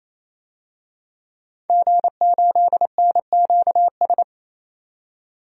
Morse code, heard as a keyed tone, about 28 wpm, 700 Hz. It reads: G8NQS
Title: DXCC-100 Call Signs - 5 Characters - 28wpm